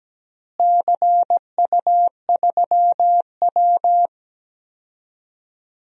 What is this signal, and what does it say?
Morse code, heard as a keyed tone, about 17 wpm, 700 Hz. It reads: CU3W
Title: DXCC-100 Call Signs - 4 Characters - Repeated 3x - 17wpm